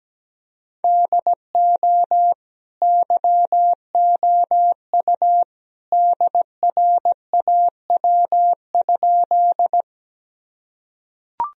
Morse code, heard as a keyed tone, about 17 wpm, 700 Hz. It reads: DO YOU DRAW?
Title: Sentences from Top 1000 Words - Review Part 1 - 17wpm